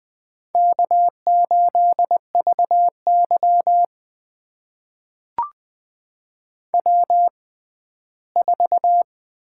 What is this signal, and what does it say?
Morse code, heard as a keyed tone, about 20 wpm, spaced as for 14 wpm, 700 Hz. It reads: K8VY W4
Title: USA Call Signs - 6x Character Spacing - 20wpm